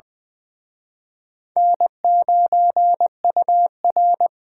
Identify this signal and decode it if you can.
Morse code, heard as a keyed tone, about 24 wpm, 700 Hz. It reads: N9UR
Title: Speed Racing - USA 1x2 Call Signs 20wpm